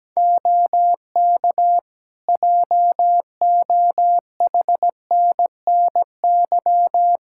Morse code, heard as a keyed tone, about 17 wpm, 700 Hz. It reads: OK JOHNNY
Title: Practice QSO - Exchange 4 - Ending - 17wpm